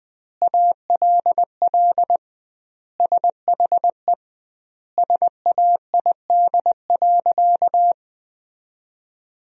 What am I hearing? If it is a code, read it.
Morse code, heard as a keyed tone, about 21 wpm, 700 Hz. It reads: ALL SHE SAID.
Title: 2x Word-Spacing - Sentences from Top 100 Words - 20wpm